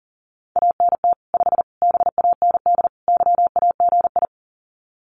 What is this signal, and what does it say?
Morse code, heard as a keyed tone, about 37 wpm, 700 Hz. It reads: ANT 5 BAND YAGI
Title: Common QSO Phrases 40wpm